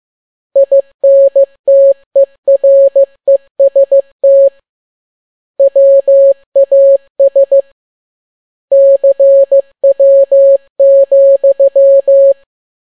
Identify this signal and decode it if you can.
Morse code, heard as a keyed tone, about 15 wpm, 550 Hz. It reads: INTEREST WAS CW,
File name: n6tt p1 15C13